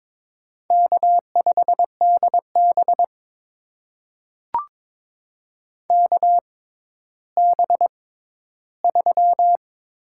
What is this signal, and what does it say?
Morse code, heard as a keyed tone, about 22 wpm, spaced as for 13 wpm, 700 Hz. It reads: K5DB KB3
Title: USA Call Signs - 6x Character Spacing - 22wpm